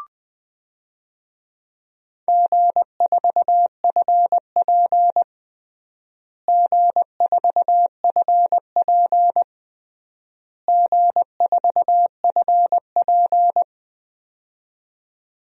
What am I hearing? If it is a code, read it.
Morse code, heard as a keyed tone, about 21 wpm, 700 Hz. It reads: G4FP G4FP G4FP
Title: DXCC-100 Call Signs - 4 Characters - Repeated 3x - 20wpm